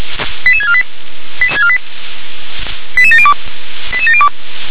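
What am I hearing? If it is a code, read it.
Five-tone selective call (ZVEI), heard as quick runs of five tones, tones of about 70 ms. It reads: 88648 88702 88702
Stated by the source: ZVEI five-tone selective call received on an SDR.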